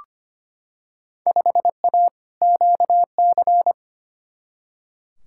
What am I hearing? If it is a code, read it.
Morse code, heard as a keyed tone, about 25 wpm, 700 Hz. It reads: 5A QC
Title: Field Day Exchanges Encore - 25wpm